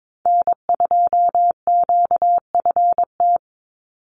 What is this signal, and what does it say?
Morse code, heard as a keyed tone, about 22 wpm, 700 Hz. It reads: N2QFT
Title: USA Call Signs 22wpm